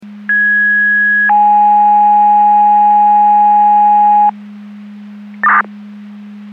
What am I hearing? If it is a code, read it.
Two-tone page, heard as a one-second tone, then a three-second tone, about 1650 then 830 Hz. It is heard from 0.3 to 4.3 s.
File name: Portage 154.385 Paging